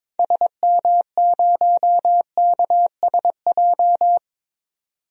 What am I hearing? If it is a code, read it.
Morse code, heard as a keyed tone, about 21 wpm, 700 Hz. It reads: SM0KSJ
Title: DXCC-100 Call Signs - 6 Characters - 22wpm